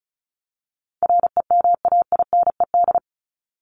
Morse code, heard as a keyed tone, about 34 wpm, 700 Hz. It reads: REMAINED
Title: ED Suffix - Top 100 Words with Base Words 35wpm